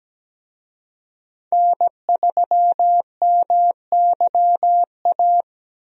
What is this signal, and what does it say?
Morse code, heard as a keyed tone, about 17 wpm, 700 Hz. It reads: N3MYA
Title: USA Call Signs - 5 Characters - Repeated 2x - 17wpm